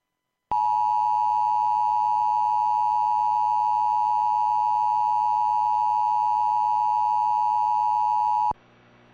Emergency Alert System (SAME) tones (the 853 and 960 Hz attention tone). It is heard from 0.5 to 8.6 s.